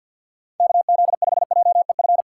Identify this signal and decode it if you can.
Morse code, heard as a keyed tone, about 49 wpm, 700 Hz. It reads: KZ5JEF
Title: USA Call Signs - 6 Characters - 6x Character Spacing - 50wpm